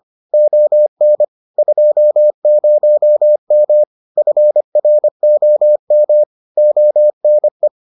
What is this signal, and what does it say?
Morse code, heard as a keyed tone, about 25 wpm, 600 Hz. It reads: ON 20M FROM ONE